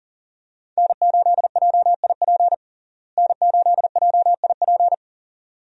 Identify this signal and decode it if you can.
Morse code, heard as a keyed tone, about 40 wpm, 700 Hz. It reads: N8JIP N8JIP
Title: USA Call Signs - 5 Characters - Repeated 2x - 40wpm